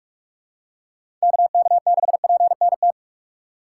Morse code, heard as a keyed tone, about 43 wpm, 700 Hz. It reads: KK6PNT
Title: USA Call Signs - 6 Characters - 6x Character Spacing - 45wpm